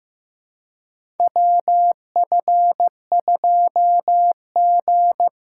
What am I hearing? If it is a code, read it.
Morse code, heard as a keyed tone, about 15 wpm, 700 Hz. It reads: WF2G
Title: USA Call Signs - 4 Characters - 2x Character Spacing - 15wpm